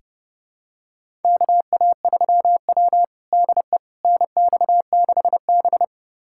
Morse code, heard as a keyed tone, about 30 wpm, 700 Hz. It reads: KA3W DE NX6B
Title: USA Callsign DE Callsign - 4 Characters - 30wpm